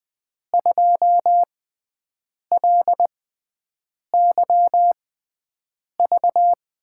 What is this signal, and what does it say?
Morse code, heard as a keyed tone, about 20 wpm, spaced as for 7 wpm, 700 Hz. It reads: 2LYV
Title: USA Call Signs - 6 Characters - 6x Character Spacing - 20wpm